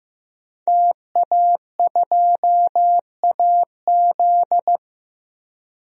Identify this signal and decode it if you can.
Morse code, heard as a keyed tone, about 15 wpm, 700 Hz. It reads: TA2AZ
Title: DXCC-100 Call Signs - 5 Characters - 15wpm